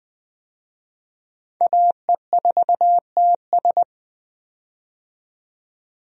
Morse code, heard as a keyed tone, about 20 wpm, 700 Hz. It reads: AE4TS
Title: USA Call Signs - 5 Characters - 20wpm